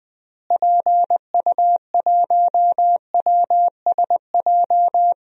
Morse code, heard as a keyed tone, about 21 wpm, 700 Hz. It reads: PU1WSJ
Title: DXCC-100 Call Signs - 6 Characters - Repeated 3x - 20wpm